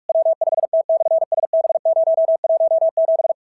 Morse code, heard as a keyed tone, about 44 wpm, 690 Hz. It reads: W5T/SB017
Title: SOTA QSO Practice 45wpm